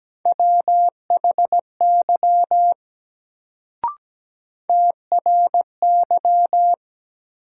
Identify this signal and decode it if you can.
Morse code, heard as a keyed tone, about 17 wpm, 700 Hz. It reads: WHY TRY
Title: Sentences from Top 200 Words - New Word Drill - 17wpm